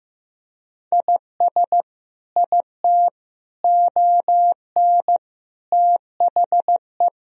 Morse code, heard as a keyed tone, about 15 wpm, 700 Hz. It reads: IS IT ON THE
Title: Sentences from Top 1000 Words - Review Part 2 - 15wpm